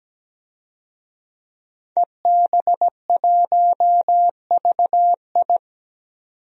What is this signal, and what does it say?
Morse code, heard as a keyed tone, about 17 wpm, 700 Hz. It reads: EB1VI
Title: DXCC-100 Call Signs - 5 Characters - 17wpm